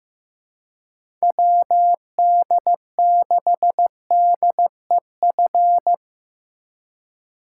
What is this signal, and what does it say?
Morse code, heard as a keyed tone, about 15 wpm, 700 Hz. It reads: WD6DEF
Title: USA Call Signs - 6 Characters - 4x Character Spacing - 15wpm